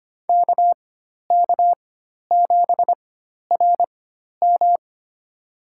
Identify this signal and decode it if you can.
Morse code, heard as a keyed tone, about 25 wpm, spaced as for 12 wpm, 700 Hz. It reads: KK7RM
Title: USA Call Signs - 5 Characters - 4x Character Spacing - 25wpm